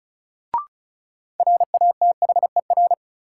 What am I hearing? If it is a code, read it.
Morse code, heard as a keyed tone, about 36 wpm, 700 Hz. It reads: RATHER
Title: Sentences from Top 700 Words - New Word Drill - 35wpm